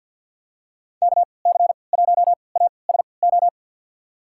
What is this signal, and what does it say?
Morse code, heard as a keyed tone, about 51 wpm, spaced as for 28 wpm, 700 Hz. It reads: KC1ASO
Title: USA Call Signs - 3x Character Spacing - 50wpm